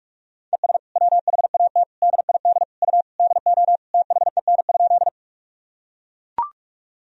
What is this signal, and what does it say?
Morse code, heard as a keyed tone, about 41 wpm, 700 Hz. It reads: ES WHAT DID U DO THEN?
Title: Sentences from Top 100 Words and Abbreviations 45wpm